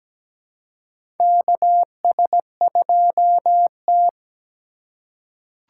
Morse code, heard as a keyed tone, about 17 wpm, 700 Hz. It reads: KS2T
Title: USA Call Signs - 6x Character Spacing - 17wpm